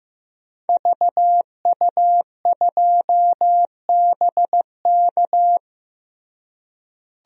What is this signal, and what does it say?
Morse code, heard as a keyed tone, about 15 wpm, 700 Hz. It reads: VU2BK
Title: DXCC-100 Call Signs - 5 Characters - Repeated 3x - 15wpm